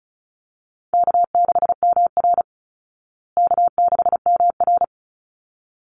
Morse code, heard as a keyed tone, about 34 wpm, 700 Hz. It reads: K6MR K6MR
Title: Easy Call Signs Repeated 2x - 35wpm